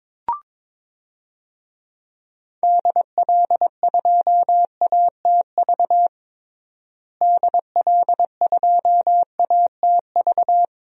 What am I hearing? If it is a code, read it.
Morse code, heard as a keyed tone, about 22 wpm, 700 Hz. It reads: DL2ATV DL2ATV
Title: DXCC-100 Call Signs - 6 Characters - Repeated 3x - 22wpm